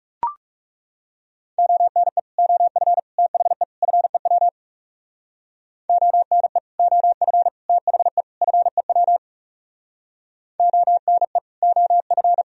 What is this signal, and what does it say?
Morse code, heard as a keyed tone, about 40 wpm, 700 Hz. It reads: ONE OF THE FEW ONE OF THE FEW ONE OF